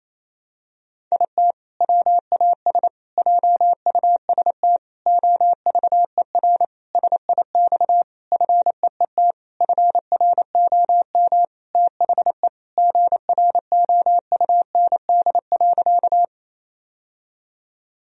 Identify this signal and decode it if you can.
Morse code, heard as a keyed tone, about 28 wpm, 700 Hz. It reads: IT WAS JUST OVER SIX FEET FROM THE GROUND.
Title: Sentences from Top 1000 Words - Review Part 2 - 28wpm